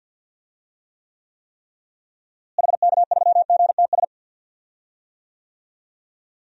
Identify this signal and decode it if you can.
Morse code, heard as a keyed tone, about 53 wpm, 700 Hz. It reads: HK3GTS
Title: DXCC-100 Call Signs - 6 Characters - Repeated 3x - 50wpm